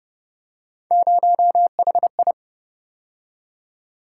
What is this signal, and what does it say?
Morse code, heard as a keyed tone, about 31 wpm, 700 Hz. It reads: 0HI